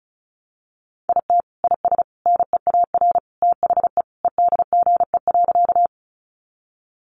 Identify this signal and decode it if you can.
Morse code, heard as a keyed tone, about 34 wpm, 700 Hz. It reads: IT IS NEAR THE EDGE.
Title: Additional Easy Phrases 35wpm